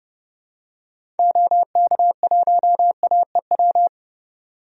Morse code, heard as a keyed tone, about 32 wpm, 700 Hz. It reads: OK1AEW
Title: DXCC-100 Call Signs - 6 Characters - 30wpm